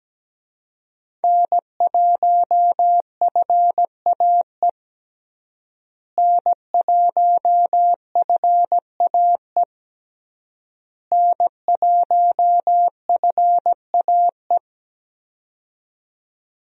Morse code, heard as a keyed tone, about 17 wpm, 700 Hz. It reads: N1FAE N1FAE N1FAE
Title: USA Call Signs - 5 Characters - Repeated 3x - 17wpm